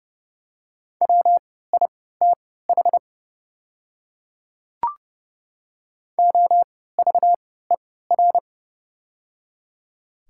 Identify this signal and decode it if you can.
Morse code, heard as a keyed tone, about 29 wpm, spaced as for 17 wpm, 690 Hz. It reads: WITH OVER
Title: 3x Character Spacing - Top 100 Words - 30wpm